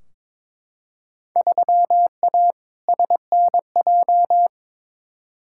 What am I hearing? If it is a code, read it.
Morse code, heard as a keyed tone, about 22 wpm, 700 Hz. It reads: 3A SNJ
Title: Field Day Exchanges - 22wpm